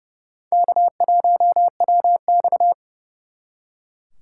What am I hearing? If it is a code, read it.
Morse code, heard as a keyed tone, about 38 wpm, 700 Hz. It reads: K1WX